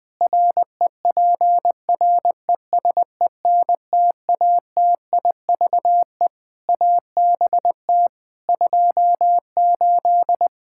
Morse code, heard as a keyed tone, about 21 wpm, 700 Hz. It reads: REPRESENTATIVE ABT 28
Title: Practice QSO - Exchange 3 - More Info - 20wpm